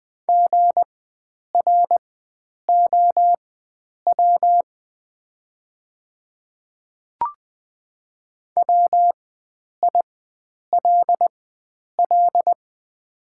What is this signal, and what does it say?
Morse code, heard as a keyed tone, about 20 wpm, spaced as for 9 wpm, 700 Hz. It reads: GROW WILL
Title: 4x Character Spacing - Top 300 Words - 20wpm